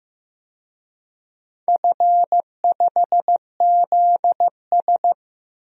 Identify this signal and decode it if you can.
Morse code, heard as a keyed tone, about 15 wpm, 700 Hz. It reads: F5ZS
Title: DXCC-100 Call Signs - 4 Characters - Repeated 2x - 15wpm